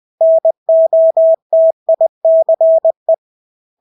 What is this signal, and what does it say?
Morse code, heard as a keyed tone, about 20 wpm, 650 Hz. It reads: NOTICE
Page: Engelske ord 20 wpm